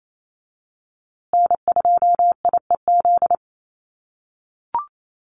Morse code, heard as a keyed tone, about 28 wpm, 700 Hz. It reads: N2IEZ
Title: USA Call Signs Repeated 3x 28wpm